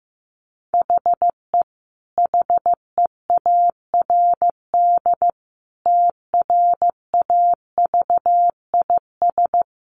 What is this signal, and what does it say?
Morse code, heard as a keyed tone, about 15 wpm, 700 Hz. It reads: HE HEARD TRAVIS
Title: A Sound Of Thunder 15wpm